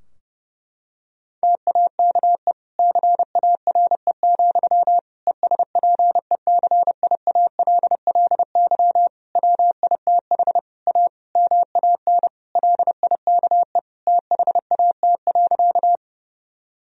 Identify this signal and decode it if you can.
Morse code, heard as a keyed tone, about 30 wpm, 700 Hz. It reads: TAKE CARE, ESPECIALLY WITH A MAN LIKE THAT.
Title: Sentences from Top 1000 Words - Review Part 4 - 30wpm